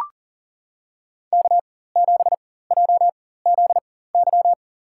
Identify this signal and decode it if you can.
Morse code, heard as a keyed tone, about 37 wpm, spaced as for 18 wpm, 700 Hz. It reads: K7JZY
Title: USA Call Signs - 4x Character Spacing - 40wpm